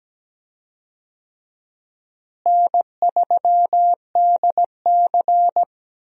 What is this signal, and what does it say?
Morse code, heard as a keyed tone, about 17 wpm, 700 Hz. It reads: N3DC
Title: USA Call Signs - 4 Characters - Repeated 2x - 17wpm